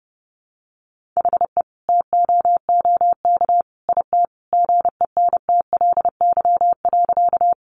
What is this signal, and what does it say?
Morse code, heard as a keyed tone, about 30 wpm, 700 Hz. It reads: HE TOOK IT GENTLY.
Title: Sentences from Top 1500 Words 30wpm